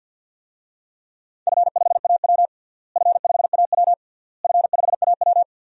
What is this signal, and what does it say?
Morse code, heard as a keyed tone, about 51 wpm, 700 Hz. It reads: F5AW F5AW F5AW
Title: DXCC-100 Call Signs - 4 Characters - Repeated 3x - 50wpm